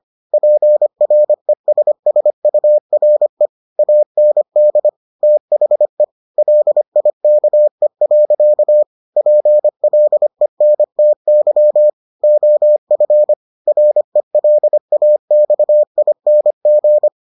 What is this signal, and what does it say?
Morse code, heard as a keyed tone, about 25 wpm, 600 Hz. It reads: PRESSURE AND THE LIKE. PLENTY OF RELAXING